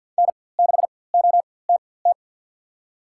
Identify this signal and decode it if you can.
Morse code, heard as a keyed tone, about 47 wpm, spaced as for 23 wpm, 700 Hz. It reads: N6OTT
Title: USA Call Signs - 5 Characters - 4x Character Spacing - 50wpm